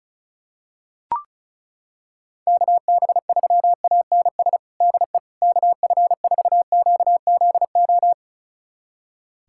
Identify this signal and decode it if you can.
Morse code, heard as a keyed tone, about 36 wpm, 700 Hz. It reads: KB3ANS DE KF4QZO
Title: USA Callsign DE Callsign - 6 Characters - 35wpm